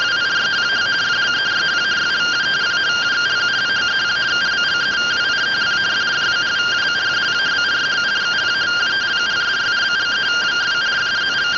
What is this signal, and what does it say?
Two-tone teleprinter signal, about 170 Hz shift.
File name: findings / axial iodine 3mm 205.gen-20 rtty rx